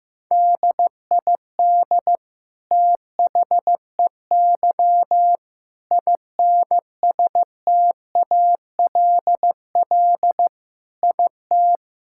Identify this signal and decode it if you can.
Morse code, heard as a keyed tone, about 15 wpm, 700 Hz. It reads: DID THEY INSTALL IT
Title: Sentences from Top 1600 Words 15wpm